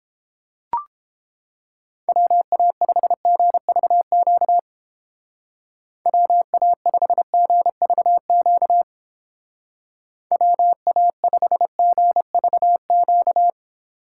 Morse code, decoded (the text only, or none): WA5GVQ WA5GVQ WA5GVQ